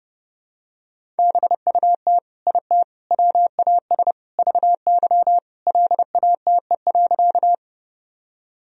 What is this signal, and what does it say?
Morse code, heard as a keyed tone, about 32 wpm, 700 Hz. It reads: BUT IT WAS VY LATE.
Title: Sentences from Top 300 Words and Abbreviations 30wpm